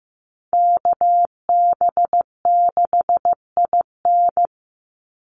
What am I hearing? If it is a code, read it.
Morse code, heard as a keyed tone, about 15 wpm, 700 Hz. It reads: KB6IN
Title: USA Call Signs 15wpm